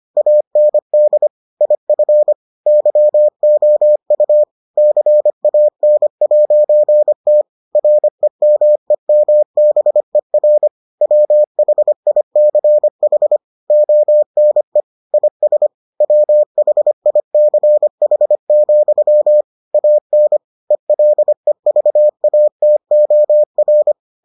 Morse code, heard as a keyed tone, about 25 wpm, 600 Hz. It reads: AND IF YOU CAN'T REMEMBER WHICH ONE IS WHICH, AN ELEVATOR